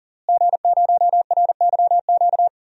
Morse code, heard as a keyed tone, about 37 wpm, 700 Hz. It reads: G0RYQ
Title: DXCC-100 Call Signs - 5 Characters - Repeated 3x - 40wpm